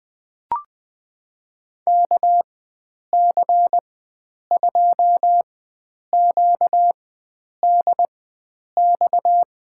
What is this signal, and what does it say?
Morse code, heard as a keyed tone, about 20 wpm, spaced as for 9 wpm, 700 Hz. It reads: KC2QDX
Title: USA Call Signs - 6 Characters - 4x Character Spacing - 20wpm